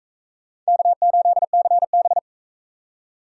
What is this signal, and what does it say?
Morse code, heard as a keyed tone, about 41 wpm, 700 Hz. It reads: K8CB